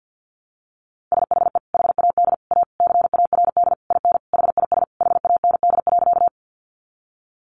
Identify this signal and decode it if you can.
Morse code, heard as a keyed tone, about 50 wpm, 700 Hz. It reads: SHE HAD A CARD IN HIS HAND.
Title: Sentences from Top 600 Words 50wpm